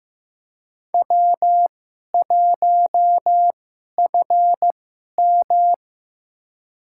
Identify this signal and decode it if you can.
Morse code, heard as a keyed tone, about 15 wpm, spaced as for 11 wpm, 700 Hz. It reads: W1FM
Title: USA Call Signs - 4 Characters - 2x Character Spacing - 15wpm